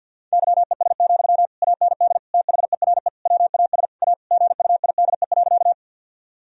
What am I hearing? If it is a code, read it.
Morse code, heard as a keyed tone, about 50 wpm, 700 Hz. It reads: YES, AND THERE WAS A GUIDE.
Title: Sentences from Top 1000 Words - Review Part 3 - 50wpm